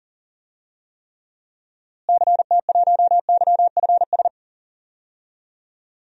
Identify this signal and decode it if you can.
Morse code, heard as a keyed tone, about 39 wpm, 700 Hz. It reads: CT1YFS
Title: DXCC-100 Call Signs - 6 Characters - Repeated 3x - 40wpm